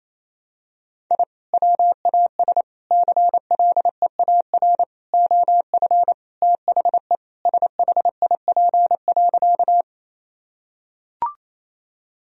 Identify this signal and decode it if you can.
Morse code, heard as a keyed tone, about 28 wpm, 700 Hz. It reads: I WAS CLEAR OF THE SHIP.
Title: Sentences from Top 1000 Words - Review Part 2 - 28wpm